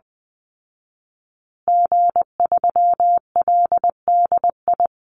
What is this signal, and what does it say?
Morse code, heard as a keyed tone, about 20 wpm, 700 Hz. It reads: G3LDI
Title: Call Signs Repeated 2x - 20wpm